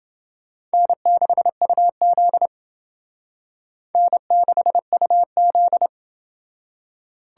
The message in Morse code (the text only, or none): N6UZ N6UZ